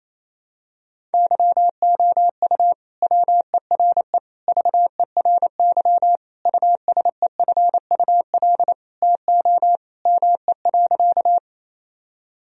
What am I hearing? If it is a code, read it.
Morse code, heard as a keyed tone, about 28 wpm, 700 Hz. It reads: YOU WERE VERY USEFUL TO ME.
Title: Sentences from Top 1000 Words - Review Part 4 - 28wpm